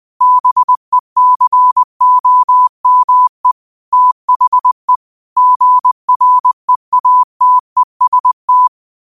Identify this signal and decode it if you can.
Morse code, heard as a keyed tone, about 20 wpm, 1000 Hz. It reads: BECOME THE GREATEST
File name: 20wpm